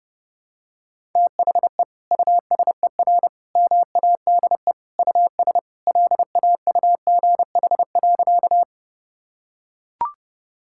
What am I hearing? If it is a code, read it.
Morse code, heard as a keyed tone, about 30 wpm, 700 Hz. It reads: THE USER MADE US LAUGH.
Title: Sentences from Top 1000 Words - Review Part 5 - 30wpm